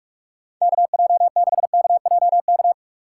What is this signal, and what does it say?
Morse code, heard as a keyed tone, about 46 wpm, 700 Hz. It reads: KJ6KJK
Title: USA Call Signs - 6 Characters - 4x Character Spacing - 45wpm